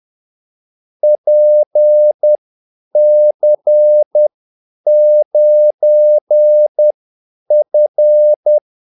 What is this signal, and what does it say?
Morse code, heard as a keyed tone, about 10 wpm, spaced as for 8 wpm, 600 Hz. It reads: PC9F
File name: Callsigns-10-8wpm